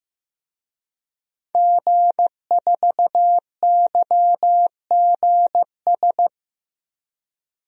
Morse code, heard as a keyed tone, about 15 wpm, 700 Hz. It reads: G4YGS
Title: DXCC-100 Call Signs - 5 Characters - 15wpm